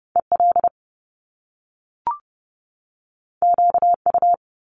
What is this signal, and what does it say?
Morse code, heard as a keyed tone, about 30 wpm, 700 Hz. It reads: EL QU
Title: Top Two Letter Combinations in Words 30wpm